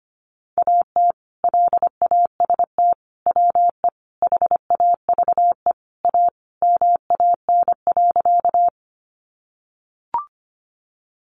Morse code, heard as a keyed tone, about 25 wpm, 700 Hz. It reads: AT LAST WE HAVE A MAN.
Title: Sentences from Top 300 Words 25wpm